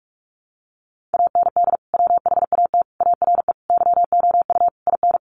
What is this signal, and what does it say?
Morse code, heard as a keyed tone, about 44 wpm, 700 Hz. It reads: AND WHAT ARE YOU IN
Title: Sentences from Top 1200 Words 45wpm